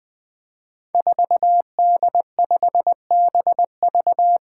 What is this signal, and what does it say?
Morse code, heard as a keyed tone, about 20 wpm, 700 Hz. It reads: 4D5BV
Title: Five Challenging Letter-Number Combinations 20wpm